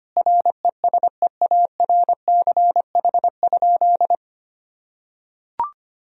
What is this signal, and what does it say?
Morse code, heard as a keyed tone, about 25 wpm, 700 Hz. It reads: RESEARCH?
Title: Sentences from Top 1000 Words - Review Part 4 - 25wpm